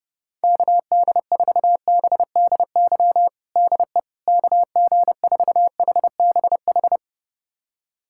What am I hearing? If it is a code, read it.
Morse code, heard as a keyed tone, about 30 wpm, 700 Hz. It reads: KD4BDY DE KG4HBH
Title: USA Callsign DE Callsign 30wpm